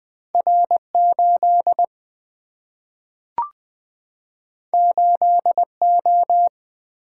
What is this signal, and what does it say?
Morse code, heard as a keyed tone, about 20 wpm, 700 Hz. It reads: R8 8O
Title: Single-Letter Single-Number 20wpm